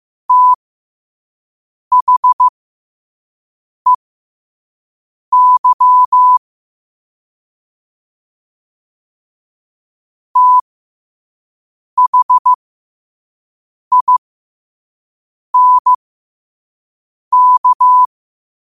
Morse code, heard as a keyed tone, about 15 wpm, spaced as for 5 wpm, 1000 Hz. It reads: THEY THINK
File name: 5wpm